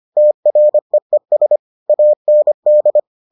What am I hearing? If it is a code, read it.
Morse code, heard as a keyed tone, about 25 wpm, 600 Hz. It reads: TREES AND